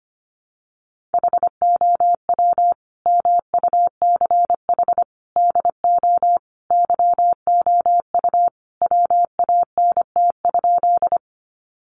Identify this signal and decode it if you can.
Morse code, heard as a keyed tone, about 25 wpm, 700 Hz. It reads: HOW MUCH DO YOU WANT?
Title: Sentences from Top 200 Words 25wpm